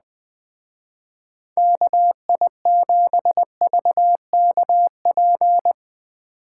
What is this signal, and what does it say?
Morse code, heard as a keyed tone, about 21 wpm, 700 Hz. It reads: KI7VKP
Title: Speed Racing - USA 2x3 Call Signs 17wpm